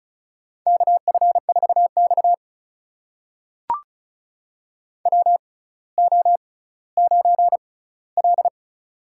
Morse code, heard as a keyed tone, about 36 wpm, spaced as for 18 wpm, 700 Hz. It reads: KF4X WO9L
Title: USA Call Signs - 4 Characters - 6x Character Spacing - 35wpm